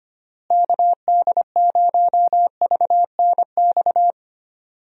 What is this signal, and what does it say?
Morse code, heard as a keyed tone, about 25 wpm, 700 Hz. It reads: KD0VNX
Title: USA Call Signs - 6 Characters - Repeated 2x - 25wpm